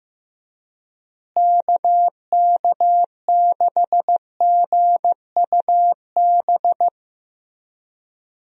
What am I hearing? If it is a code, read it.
Morse code, heard as a keyed tone, about 15 wpm, 700 Hz. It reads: KK6GUB
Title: USA Call Signs - 6 Characters - Repeated 3x - 15wpm